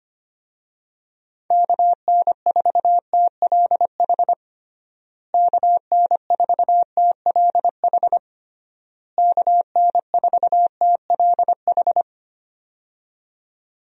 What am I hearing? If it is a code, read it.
Morse code, heard as a keyed tone, about 25 wpm, 700 Hz. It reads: KN4TLH KN4TLH KN4TLH
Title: USA Call Signs - 6 Characters - Repeated 3x - 25wpm